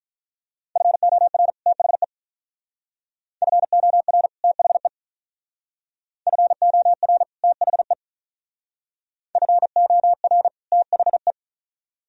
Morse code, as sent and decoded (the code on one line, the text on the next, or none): ..-. --- .-. / - .... / ..-. --- .-. / - .... / ..-. --- .-. / - .... / ..-. --- .-. / - .... .
FOR THE FOR THE FOR THE FOR THE